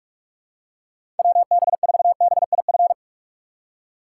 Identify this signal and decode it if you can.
Morse code, heard as a keyed tone, about 46 wpm, 700 Hz. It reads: WB4BIF
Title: USA Call Signs - 6 Characters - 4x Character Spacing - 45wpm